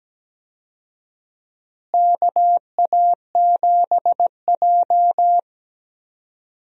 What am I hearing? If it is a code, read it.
Morse code, heard as a keyed tone, about 17 wpm, 700 Hz. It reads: KA7J
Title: USA Call Signs - 4 Characters - Repeated 2x - 17wpm